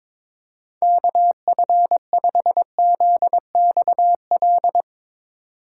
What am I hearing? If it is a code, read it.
Morse code, heard as a keyed tone, about 22 wpm, 700 Hz. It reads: KF5ZXL
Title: USA Call Signs - 6 Characters - 4x Character Spacing - 22wpm